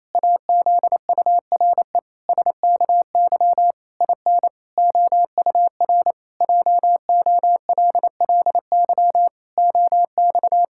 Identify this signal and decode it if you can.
Morse code, heard as a keyed tone, about 28 wpm, 700 Hz. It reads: AZURE SKY IN OUR JOLLY OX
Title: Pangram Challenge v2 - 28wpm